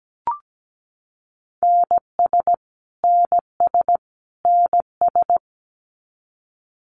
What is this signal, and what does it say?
Morse code, heard as a keyed tone, about 17 wpm, 700 Hz. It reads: NS NS NS
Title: Canadian Provinces and Territories - Repeated 3x - 17wpm